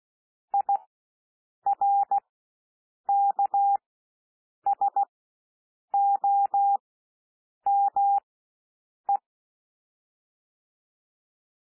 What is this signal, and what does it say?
Morse code, heard as a keyed tone, about 16 wpm, spaced as for 7 wpm, 800 Hz. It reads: IRKSOME
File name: SOUND WORDS 16 wpm